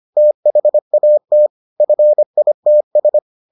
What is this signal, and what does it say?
Morse code, heard as a keyed tone, about 25 wpm, 600 Hz. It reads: THAT FITS